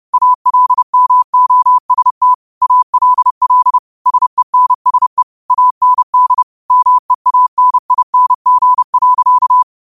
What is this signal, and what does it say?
Morse code, heard as a keyed tone, about 30 wpm, 1000 Hz. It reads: ALMOST ALL SENSE AND MEANING.
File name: 30wpm